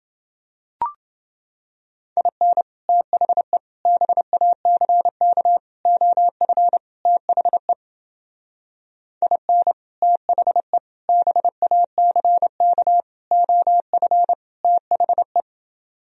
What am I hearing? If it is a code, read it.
Morse code, heard as a keyed tone, about 29 wpm, 700 Hz. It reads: IN THE BACK OF THE IN THE BACK OF THE